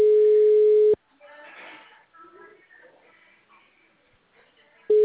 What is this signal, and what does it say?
Ringback tone (Europe, 425 Hz).